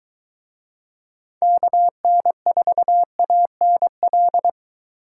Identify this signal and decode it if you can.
Morse code, heard as a keyed tone, about 22 wpm, 700 Hz. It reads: KN4ANL